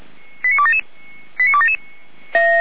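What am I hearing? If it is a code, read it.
Five-tone selective call (ZVEI), heard as quick runs of five tones, tones of about 70 ms. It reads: 89277 89277
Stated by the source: ZVEI five-tone selective call received with an SDR.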